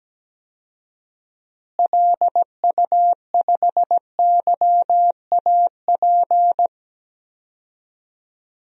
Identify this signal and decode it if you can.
Morse code, heard as a keyed tone, about 17 wpm, 700 Hz. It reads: LU5YAP
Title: DXCC-100 Call Signs - 6 Characters - Repeated 3x - 17wpm